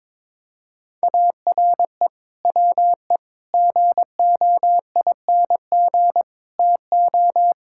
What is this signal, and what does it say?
Morse code, heard as a keyed tone, about 22 wpm, 700 Hz. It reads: ARE WE GOING TO
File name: practice / 22wpm / Sentences with 4-Word Phrases - 22wpm